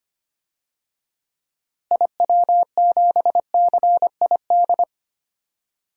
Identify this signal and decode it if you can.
Morse code, heard as a keyed tone, about 25 wpm, 700 Hz. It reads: IW7CID
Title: DXCC-100 Call Signs - 6 Characters - Repeated 2x - 25wpm